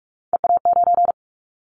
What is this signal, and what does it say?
Morse code, heard as a keyed tone, about 44 wpm, 700 Hz. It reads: EA9
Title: Callsign Prefixes for Top 100 DXCC - 45wpm